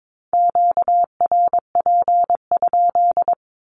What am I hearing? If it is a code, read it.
Morse code, heard as a keyed tone, about 22 wpm, 700 Hz. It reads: QRP?
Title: Q-Codes 22wpm